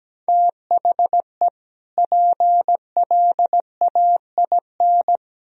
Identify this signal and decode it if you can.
Morse code, heard as a keyed tone, about 17 wpm, 700 Hz. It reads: THE PLAIN
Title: Sentences from Top 1000 Words - Review Part 2 - 17wpm